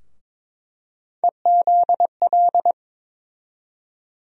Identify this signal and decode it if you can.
Morse code, heard as a keyed tone, about 21 wpm, 700 Hz. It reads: EZL
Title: USA Call Sign Postfixes - 3 Letters 22wpm